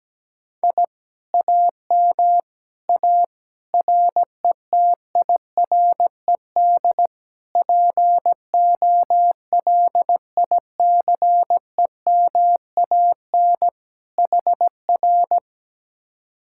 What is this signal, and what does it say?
Morse code, heard as a keyed tone, about 17 wpm, 700 Hz. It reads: I AM A RETIRED POLICEMAN HR
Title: Speed-Racing Common QSO Phrases Encore - 17wpm